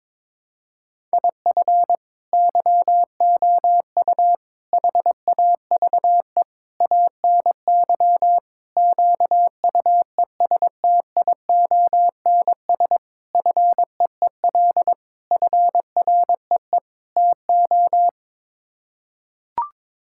Morse code, heard as a keyed tone, about 22 wpm, 700 Hz. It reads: IF YOU HAVE ANY QUESTIONS FEEL FREE TO
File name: practice / 22wpm / Sets of 8 Words - 22wpm